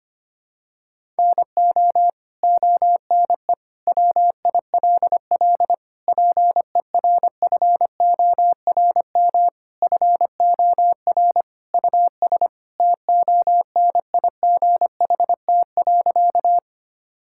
Morse code, heard as a keyed tone, about 25 wpm, 700 Hz. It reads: NO ONE WILL PERFORM FOR US TONIGHT.
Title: Sentences from Top 1000 Words - Review Part 5 - 25wpm